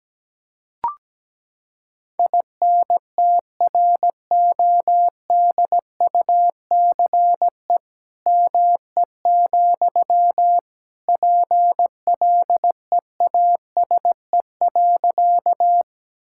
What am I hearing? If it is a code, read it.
Morse code, heard as a keyed tone, about 17 wpm, 700 Hz. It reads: INTRODUCE ME, PLEASE.
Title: Sentences from Top 1600 Words 17wpm